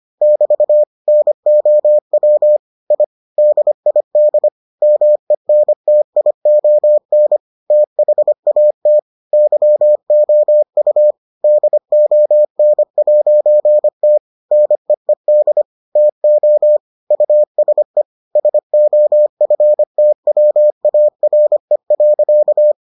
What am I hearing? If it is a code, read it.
Morse code, heard as a keyed tone, about 25 wpm, 600 Hz. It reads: <BT> NOW I DID MENTION THAT YOU DON'T NEED TO USE SOFTWARE.